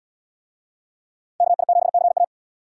Morse code, heard as a keyed tone, about 69 wpm, 700 Hz. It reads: KI7CA